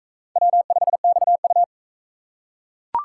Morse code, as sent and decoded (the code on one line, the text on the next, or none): .-- ..... -..- ..-
W5XU